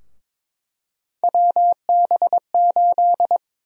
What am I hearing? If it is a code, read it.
Morse code, heard as a keyed tone, about 21 wpm, 700 Hz. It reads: WB8
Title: USA Call Sign Prefixes - 22wpm